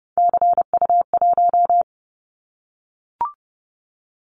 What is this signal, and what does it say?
Morse code, heard as a keyed tone, about 30 wpm, 700 Hz. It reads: CU1
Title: Callsign Prefixes for Top 50 DXCC - 30wpm